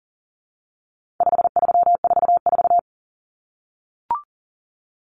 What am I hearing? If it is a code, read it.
Morse code, heard as a keyed tone, about 40 wpm, 700 Hz. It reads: 5344
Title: Numbers - 4-Digit 40wpm